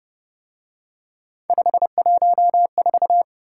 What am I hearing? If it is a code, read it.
Morse code, heard as a keyed tone, about 31 wpm, 700 Hz. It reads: 514
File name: Speed-Racing - Three Letter-Number Combinations - 25wpm